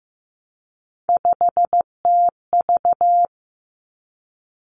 Morse code, heard as a keyed tone, about 15 wpm, 700 Hz. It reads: 5TV
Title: Three Challenging Letter-Number Combinations 15wpm